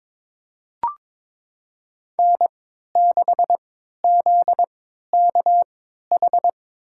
Morse code, heard as a keyed tone, about 22 wpm, spaced as for 13 wpm, 700 Hz. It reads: N6ZKH
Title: USA Call Signs - 5 Characters - 3x Character Spacing - 22wpm